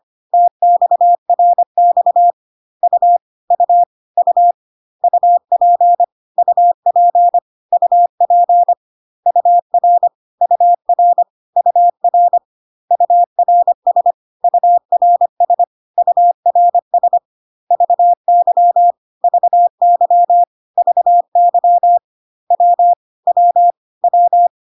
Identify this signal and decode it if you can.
Morse code, heard as a keyed tone, about 25 wpm, 700 Hz. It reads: TXRX U U U UP UP UP UR UR UR URS URS URS VY VY VY W W W
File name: cw / CW common abbreviations alphabetical repeated 3 times 25 25wpm ik1hge